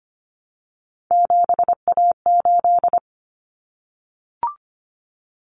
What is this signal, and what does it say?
Morse code, heard as a keyed tone, about 25 wpm, 700 Hz. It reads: 7A8
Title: Common Cut Numbers - 3-Digit 25wpm